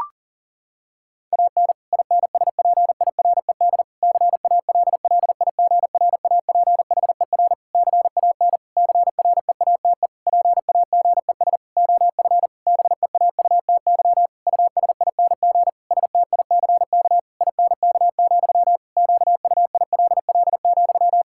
Morse code, heard as a keyed tone, about 40 wpm, 700 Hz. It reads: AN INSPIRED CALLIGRAPHER CAN CREATE PAGES OF BEAUTY USING STICK INK, QUILL,
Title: Pangram Challenge v2 - 40wpm